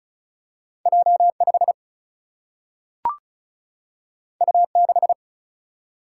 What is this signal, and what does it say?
Morse code, heard as a keyed tone, about 36 wpm, 700 Hz. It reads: J 5 U 6
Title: Single-Letter Single-Number 35wpm